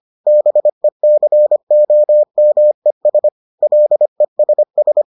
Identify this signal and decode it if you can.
Morse code, heard as a keyed tone, about 25 wpm, 600 Hz. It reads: BECOMES LESS